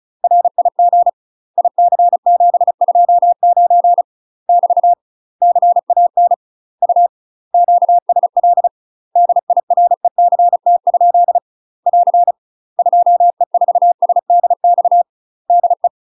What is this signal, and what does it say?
Morse code, heard as a keyed tone, about 36 wpm, 700 Hz. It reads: RIG IC729 <BT> CAN U QSL DIRECT? <AR> 2E4SDX DE
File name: UK MASTER QSOs TRAINING / 35WPM / UK MASTER QSOs 0032